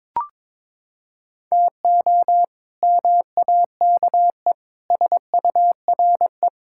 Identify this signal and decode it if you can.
Morse code, heard as a keyed tone, about 22 wpm, 700 Hz. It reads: TO MAKE SURE
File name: practice / 22wpm / Sentences with 3-Word Phrases - 22wpm